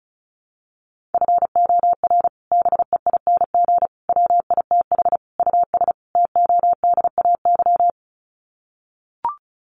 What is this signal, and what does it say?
Morse code, heard as a keyed tone, about 34 wpm, 700 Hz. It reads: FOR BEING WITH US TODAY
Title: Sets of 5 Words Encore2 35wpm